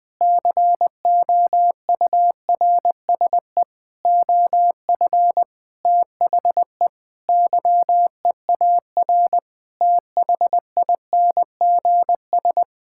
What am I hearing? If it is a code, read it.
Morse code, heard as a keyed tone, about 21 wpm, 700 Hz. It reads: COURSE OF THE YEAR THINGS
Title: Sentences with 5-Word Phrases - 20wpm